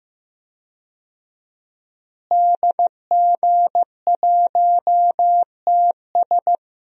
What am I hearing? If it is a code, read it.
Morse code, heard as a keyed tone, about 15 wpm, 700 Hz. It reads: DG1TS
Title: DXCC-100 Call Signs - 5 Characters - Repeated 2x - 15wpm